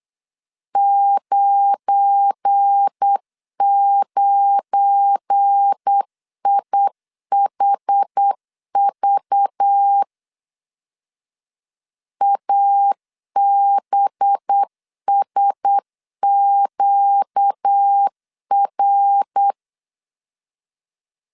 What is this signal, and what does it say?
Morse code, heard as a keyed tone, about 8 wpm, 800 Hz. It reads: 99IHV ABSQR
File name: LEZIONI / 7 GF9Z / 7 GF9 8wpm 3545